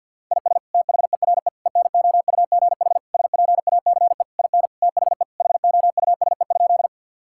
Morse code, heard as a keyed tone, about 52 wpm, 700 Hz. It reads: IS THERE ENOUGH SPACE IN THE HOUSE?
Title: Sentences from Top 1000 Words - Review Part 5 - 50wpm